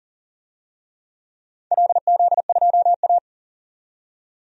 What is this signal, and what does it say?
Morse code, heard as a keyed tone, about 41 wpm, 700 Hz. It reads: LZ2A
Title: DXCC-100 Call Signs - 4 Characters - 40wpm